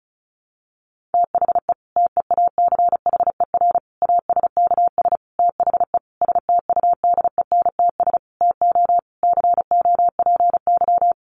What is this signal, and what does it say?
Morse code, heard as a keyed tone, about 34 wpm, 700 Hz. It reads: THE TEACHER ASKS THE STUDENTS TO COPY
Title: Additional Easy Phrases 35wpm